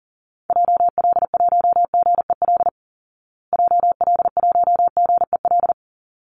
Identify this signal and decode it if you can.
Morse code, heard as a keyed tone, about 37 wpm, 700 Hz. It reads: JL1GEL JL1GEL
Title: Call Signs Repeated 3x - 40wpm